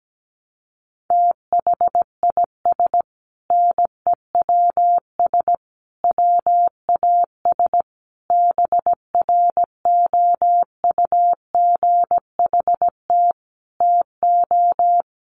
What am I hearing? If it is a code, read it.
Morse code, heard as a keyed tone, about 17 wpm, 700 Hz. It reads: THIS NEWS WAS BROUGHT TO
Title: Sentences from Top 800 Words 17wpm